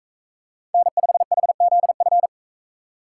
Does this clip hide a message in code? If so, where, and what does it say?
Yes: Morse code from 0.7 to 2.3 s, heard as a keyed tone, about 44 wpm, 700 Hz. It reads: N5HZF